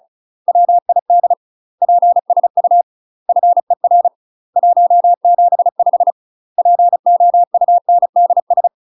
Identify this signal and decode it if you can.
Morse code, heard as a keyed tone, about 36 wpm, 700 Hz. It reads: WID PSU FER 175 POUNDS
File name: UK MASTER QSOs TRAINING / 35WPM / UK MASTER QSOs 0022